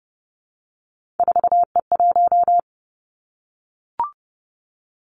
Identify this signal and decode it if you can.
Morse code, heard as a keyed tone, about 30 wpm, 700 Hz. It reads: 4E1
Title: Three Challenging Letter-Number Combinations 30wpm